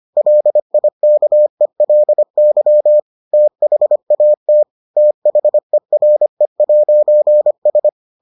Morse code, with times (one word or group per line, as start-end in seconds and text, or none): 0.2-3.0 LIKELY
3.3-4.6 THAT
5.0-7.9 THERE'S